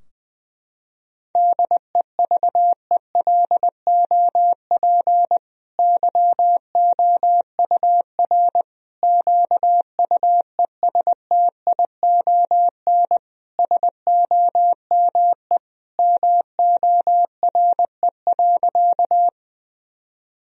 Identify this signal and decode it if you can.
Morse code, heard as a keyed tone, about 20 wpm, 700 Hz. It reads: DEVELOP YOUR QUESTION SOME MORE.
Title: Sentences from Top 1000 Words - Review Part 2 - 20wpm